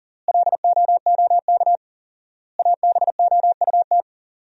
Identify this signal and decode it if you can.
Morse code, heard as a keyed tone, about 37 wpm, 700 Hz. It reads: LOOK ABOUT
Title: 4x Word-Spacing - Sentences from Top 200 Words - 40wpm